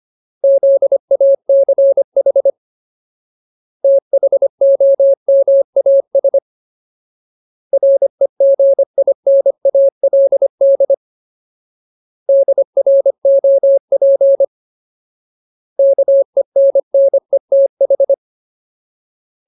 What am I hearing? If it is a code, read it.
Morse code, heard as a keyed tone, about 25 wpm, 550 Hz. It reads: ZACH THOMAS REGINALD DROP KENNETH
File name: Ultimate Word List / 25WPM Ultimate